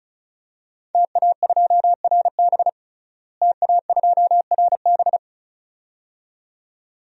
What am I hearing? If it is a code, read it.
Morse code, heard as a keyed tone, about 36 wpm, 700 Hz. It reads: TA2RB TA2RB
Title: DXCC-100 Call Signs - 5 Characters - Repeated 2x - 35wpm